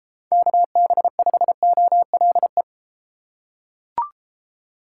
Morse code, heard as a keyed tone, about 32 wpm, 700 Hz. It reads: KB5OLE